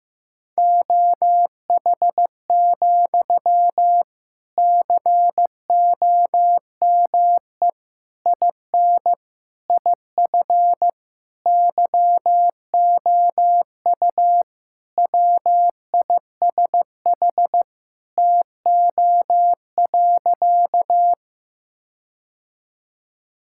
Morse code, heard as a keyed tone, about 15 wpm, 700 Hz. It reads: OH, COME IN IF YOU WISH TO.
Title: Sentences from Top 1600 Words 15wpm